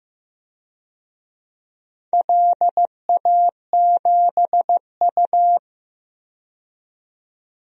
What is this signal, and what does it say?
Morse code, heard as a keyed tone, about 15 wpm, 700 Hz. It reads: LA7U
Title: DXCC-100 Call Signs - 4 Characters - 15wpm